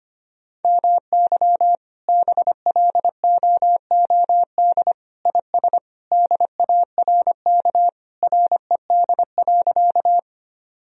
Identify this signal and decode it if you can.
Morse code, heard as a keyed tone, about 25 wpm, 700 Hz. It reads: MY BLOOD IS DARK RED.
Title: Sentences from Top 1000 Words - Review Part 2 - 25wpm